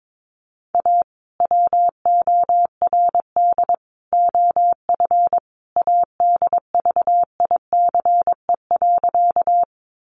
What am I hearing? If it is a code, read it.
Morse code, heard as a keyed tone, about 22 wpm, 700 Hz. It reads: A WORD OF ADVICE.
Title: Sentences from Top 1100 Words 22wpm